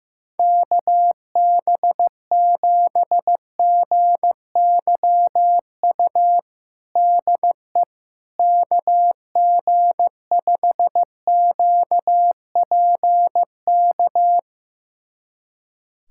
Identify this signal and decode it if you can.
Morse code, heard as a keyed tone, about 16 wpm, 700 Hz. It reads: KB7GYU DE KG5QPK
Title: USA Callsign DE Callsign - 6 Characters - 15wpm